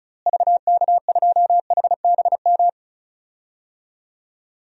Morse code, heard as a keyed tone, about 36 wpm, 700 Hz. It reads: VK2HBM
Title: DXCC-100 Call Signs - 6 Characters - Repeated 3x - 35wpm